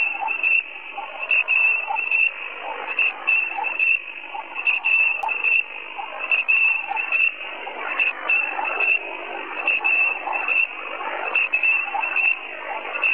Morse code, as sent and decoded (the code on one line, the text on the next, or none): none